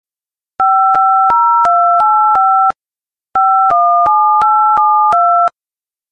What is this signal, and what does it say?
DTMF, heard as telephone touch tones, tones of about 360 ms. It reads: 50285 517872
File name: games / beeps3